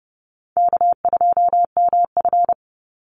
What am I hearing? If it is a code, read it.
Morse code, heard as a keyed tone, about 30 wpm, 700 Hz. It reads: K2MF
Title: Easy Call Signs Repeated 2x - 30wpm